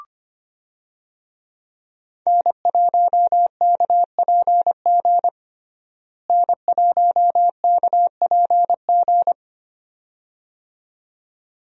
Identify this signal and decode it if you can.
Morse code, heard as a keyed tone, about 25 wpm, 700 Hz. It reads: N1KPG N1KPG
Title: USA Call Signs - 5 Characters - Repeated 2x - 25wpm